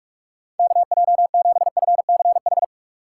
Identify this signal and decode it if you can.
Morse code, heard as a keyed tone, about 44 wpm, 700 Hz. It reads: KJ7FCH